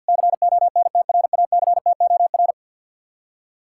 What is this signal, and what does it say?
Morse code, heard as a keyed tone, about 66 wpm, spaced as for 57 wpm, 700 Hz. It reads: CONTRACTOR
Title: Speed-Racing Common QSO Phrases - 50wpm